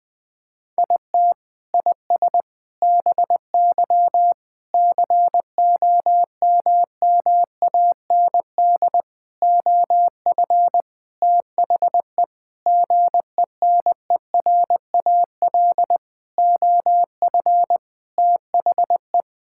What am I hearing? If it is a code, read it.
Morse code, heard as a keyed tone, about 20 wpm, 700 Hz. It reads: IT IS BY COMMAND OF THE GENERAL OF THE
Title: Sentences from Top 1600 Words 20wpm